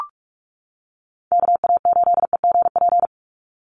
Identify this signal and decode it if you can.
Morse code, heard as a keyed tone, about 44 wpm, 700 Hz. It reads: KA8EGP
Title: USA Call Signs 45wpm